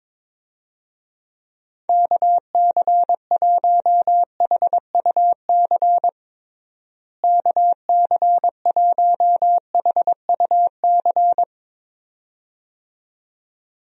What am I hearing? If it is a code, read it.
Morse code, heard as a keyed tone, about 22 wpm, 700 Hz. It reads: KC1HUC KC1HUC
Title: USA Call Signs - 6 Characters - Repeated 2x - 22wpm